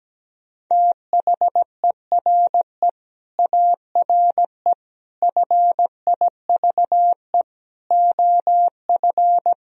Morse code, heard as a keyed tone, about 17 wpm, 700 Hz. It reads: THERE ARE FIVE OF
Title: Sentences from Top 1000 Words - Review Part 2 - 17wpm